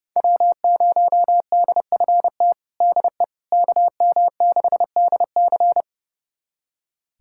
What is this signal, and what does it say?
Morse code, heard as a keyed tone, about 30 wpm, 700 Hz. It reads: W0DFT DE KM6DC
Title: USA Callsign DE Callsign - 5 Characters - 30wpm